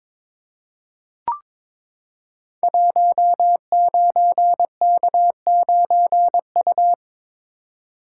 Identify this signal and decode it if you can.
Morse code, heard as a keyed tone, about 22 wpm, 700 Hz. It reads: 19K9U
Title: Code Groups - 5 Letter:Numbers - 22wpm